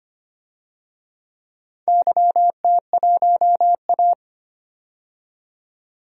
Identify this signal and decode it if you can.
Morse code, heard as a keyed tone, about 25 wpm, 690 Hz. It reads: YT1A
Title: DXCC-100 Call Signs - 4 Characters - 25wpm